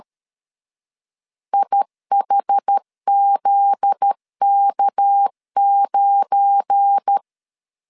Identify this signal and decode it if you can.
Morse code, heard as a keyed tone, about 13 wpm, 800 Hz. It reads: IHZK9
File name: CW / DET / 1 9 1500C S B 12wpm 3819